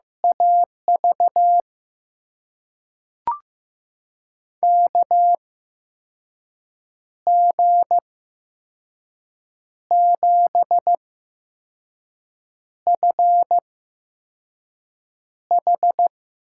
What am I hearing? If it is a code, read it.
Morse code, heard as a keyed tone, about 15 wpm, spaced as for 5 wpm, 700 Hz. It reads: AV KG7FH
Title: USA Call Signs - 6 Characters - 8x Character Spacing - 15wpm